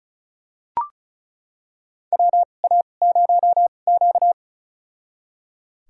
Morse code, heard as a keyed tone, about 34 wpm, spaced as for 25 wpm, 700 Hz. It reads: WA0Q
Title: USA Call Signs - 4 Characters - 2x Character Spacing - 35wpm